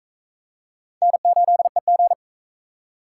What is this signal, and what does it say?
Morse code, heard as a keyed tone, about 40 wpm, 700 Hz. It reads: N8EG